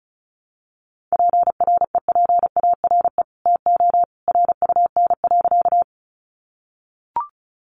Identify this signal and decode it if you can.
Morse code, heard as a keyed tone, about 34 wpm, 700 Hz. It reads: PREPARE TO RUN.
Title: Sentences from Top 900 Words 35wpm